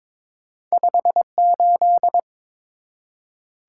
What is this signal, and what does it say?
Morse code, heard as a keyed tone, about 30 wpm, spaced as for 26 wpm, 700 Hz. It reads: 58
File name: Speed-Racing - Two Letter-Number Combinations - 22wpm